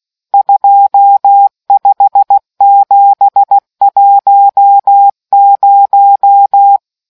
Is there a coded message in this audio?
Yes: Morse code from 0.3 to 6.8 s, heard as a keyed tone, about 16 wpm, 800 Hz. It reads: 25710